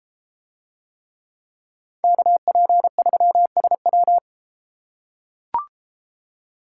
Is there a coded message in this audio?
Yes: Morse code from 2.0 to 4.2 s, heard as a keyed tone, about 33 wpm, 700 Hz. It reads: KP3SW